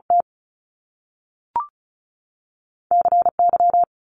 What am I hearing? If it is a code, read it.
Morse code, heard as a keyed tone, about 36 wpm, spaced as for 6 wpm, 700 Hz. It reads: TCY
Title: Top Two Letter Combinations in Words 35wpm